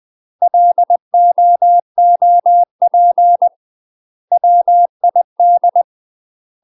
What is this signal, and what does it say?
Morse code, heard as a keyed tone, about 20 wpm, 700 Hz. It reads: LOOP WID
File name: UK MASTER QSOs TRAINING / 20WPM / UK MASTER QSOs 0059